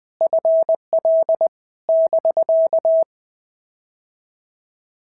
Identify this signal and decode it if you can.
Morse code, heard as a keyed tone, about 20 wpm, 650 Hz. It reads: FL <BK>
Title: POTA QSO Practice - 20wpm